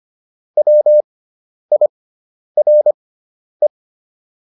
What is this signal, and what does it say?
Morse code, heard as a keyed tone, about 25 wpm, spaced as for 10 wpm, 600 Hz. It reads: WIRE